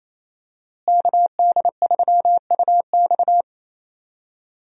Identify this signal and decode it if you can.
Morse code, heard as a keyed tone, about 28 wpm, 700 Hz. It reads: KD3UX
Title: USA Call Signs - 5 Characters - Repeated 3x - 28wpm